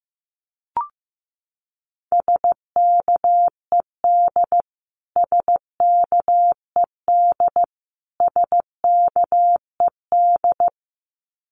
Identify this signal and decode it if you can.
Morse code, heard as a keyed tone, about 14 wpm, 700 Hz. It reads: SKED SKED SKED
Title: Abbreviations Repeated 3x - 15wpm